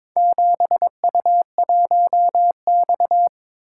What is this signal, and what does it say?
Morse code, heard as a keyed tone, about 22 wpm, 700 Hz. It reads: 7U1X
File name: practice / 22wpm / Four Letter-Number Combinations - 22wpm